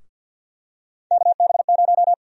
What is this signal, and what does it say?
Morse code, heard as a keyed tone, about 50 wpm, 700 Hz. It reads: KB0
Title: USA Call Sign Prefixes - 50wpm